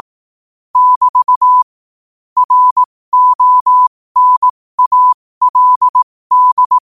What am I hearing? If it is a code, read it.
Morse code, heard as a keyed tone, about 18 wpm, spaced as for 15 wpm, 1000 Hz. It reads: <BT> RONALD
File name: qotdfiles / 15wpm